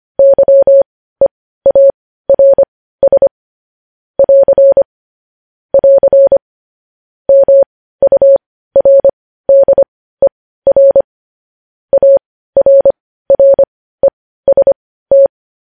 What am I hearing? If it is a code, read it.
Morse code, heard as a keyed tone, about 25 wpm, spaced as for 15 wpm, 550 Hz. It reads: YEARS <AR> <AR> MURDER ARREST
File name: podcast / News headlines-1x-25x15